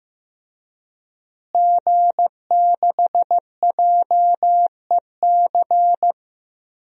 Morse code, heard as a keyed tone, about 15 wpm, 700 Hz. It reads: G6JEC
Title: DXCC-100 Call Signs - 5 Characters - Repeated 2x - 15wpm